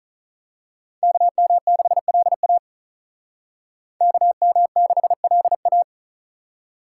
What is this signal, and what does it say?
Morse code, heard as a keyed tone, about 39 wpm, 700 Hz. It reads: KM6LA KM6LA
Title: Speed Racing - USA 2x2 Call Signs 35wpm